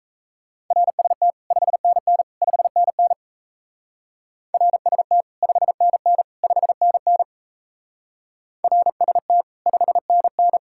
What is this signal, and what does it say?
Morse code, heard as a keyed tone, about 36 wpm, 700 Hz. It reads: RST 5NN 5NN RST 5NN 5NN RST 5NN